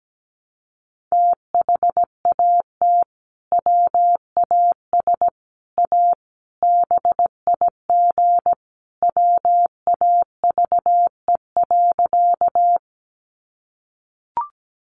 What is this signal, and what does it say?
Morse code, heard as a keyed tone, about 17 wpm, 700 Hz. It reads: THAT WAS A BIG WAVE.
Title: Sentences from Top 1400 Words 17wpm